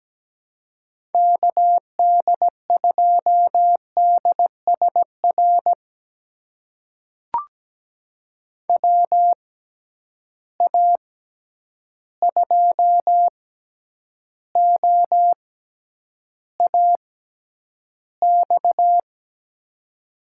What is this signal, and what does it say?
Morse code, heard as a keyed tone, about 17 wpm, spaced as for 9 wpm, 700 Hz. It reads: KD2DSR WA2OAX
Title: USA Call Signs - 6 Characters - 6x Character Spacing - 17wpm